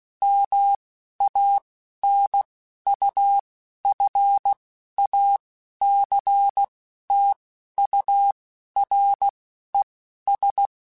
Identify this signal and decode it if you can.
Morse code, heard as a keyed tone, about 16 wpm, spaced as for 11 wpm, 800 Hz. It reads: MANUFACTURES
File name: Chuckles / Chuckles-mp3s / Guaranteed @10